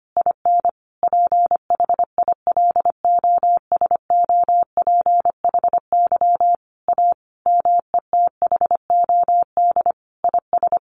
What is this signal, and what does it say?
Morse code, heard as a keyed tone, about 25 wpm, 700 Hz. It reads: IN PHILOSOPHY A METHOD IS
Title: Sentences from Top 1400 Words 25wpm